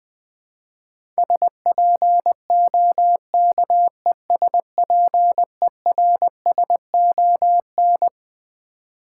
Morse code, heard as a keyed tone, about 20 wpm, 700 Hz. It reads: SPOKESPERSON
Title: Build A Word - 6 Letters At A Time - 12-Letter Words - 20wpm